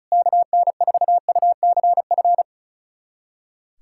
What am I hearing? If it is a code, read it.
Morse code, heard as a keyed tone, about 36 wpm, 700 Hz. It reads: KN4UCF
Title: USA Callsign DE Callsign 35wpm